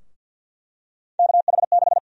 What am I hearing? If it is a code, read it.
Morse code, heard as a keyed tone, about 49 wpm, 700 Hz. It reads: KH6
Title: USA Call Sign Prefixes - 50wpm